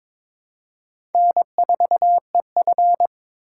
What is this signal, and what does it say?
Morse code, heard as a keyed tone, about 22 wpm, 700 Hz. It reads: N4EF
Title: USA Call Signs - 4x Character Spacing - 22wpm